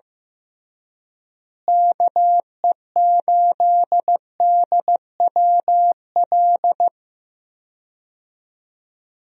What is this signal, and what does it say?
Morse code, heard as a keyed tone, about 16 wpm, 700 Hz. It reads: KE8DWL
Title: USA Call Signs - 6 Characters - Repeated 3x - 15wpm